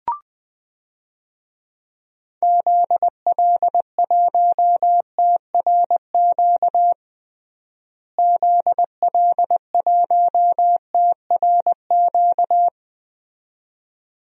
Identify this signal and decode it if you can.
Morse code, heard as a keyed tone, about 20 wpm, 700 Hz. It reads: ZL1TRQ ZL1TRQ
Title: DXCC-100 Call Signs - 6 Characters - Repeated 2x - 20wpm